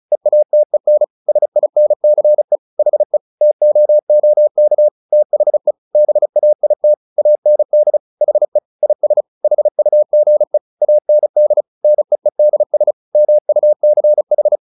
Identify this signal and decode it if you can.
Morse code, heard as a keyed tone, about 34 wpm, 600 Hz. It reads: EATEN SINCE HE TOOK THE BAIT AND HE IS HUGE AND NEEDS MUCH
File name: the old man and the sea 2 35wpm